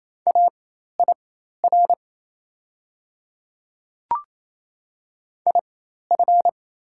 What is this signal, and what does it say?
Morse code, heard as a keyed tone, about 27 wpm, spaced as for 13 wpm, 700 Hz. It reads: AIR IF
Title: 4x Character Spacing - Top 300 Words - 28wpm